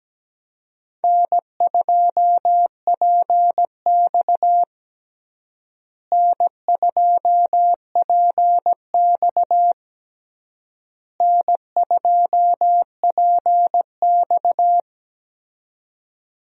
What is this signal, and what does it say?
Morse code, heard as a keyed tone, about 17 wpm, 700 Hz. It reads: N2PX N2PX N2PX
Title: USA Call Signs - 4 Characters - Repeated 3x - 17wpm